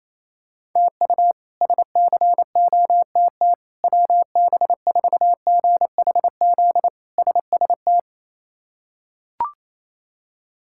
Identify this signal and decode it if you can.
Morse code, heard as a keyed tone, about 27 wpm, 700 Hz. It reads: TU SCOTT WB4GHZ SST
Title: K1USN SST QSO Practice - 28wpm